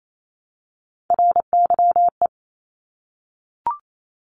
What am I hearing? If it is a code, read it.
Morse code, heard as a keyed tone, about 28 wpm, 700 Hz. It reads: RYE
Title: Multi-letter Phonemes Repeated 3x 28wpm